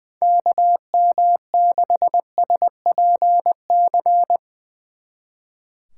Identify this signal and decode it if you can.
Morse code, heard as a keyed tone, about 22 wpm, 700 Hz. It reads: KM6SPC
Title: Speed Racing - USA Call Signs 20wpm